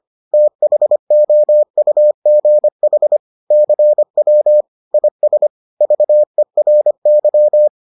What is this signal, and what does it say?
Morse code, heard as a keyed tone, about 25 wpm, 600 Hz. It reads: THOUGH CW IS VERY